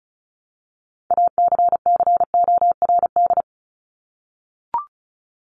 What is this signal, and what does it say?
Morse code, heard as a keyed tone, about 34 wpm, 700 Hz. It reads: ACCORD
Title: ING Suffix - Top 100 Words with Base Words 35wpm